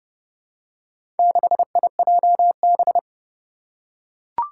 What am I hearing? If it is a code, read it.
Morse code, heard as a keyed tone, about 30 wpm, 700 Hz. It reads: 6IJB
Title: Four Challenging Letter-Number Combinations 30wpm